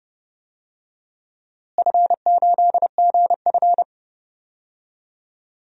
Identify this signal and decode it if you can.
Morse code, heard as a keyed tone, about 30 wpm, 700 Hz. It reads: F8GF
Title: DXCC-100 Call Signs - 4 Characters - Repeated 2x - 30wpm